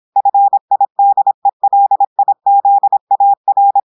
Morse code, heard as a keyed tone, about 26 wpm, 800 Hz. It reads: FIDELIZAR
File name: portugais texte en clair 2 vitesse 26 mots minutes